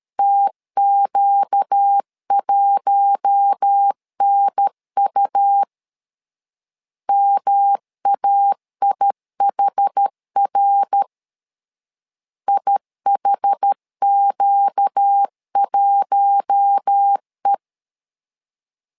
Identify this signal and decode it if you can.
Morse code, heard as a keyed tone, about 13 wpm, 800 Hz. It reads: TQ1NU MAIHR IHQ1E
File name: LEZIONI / 4 Q1 12wpm 2028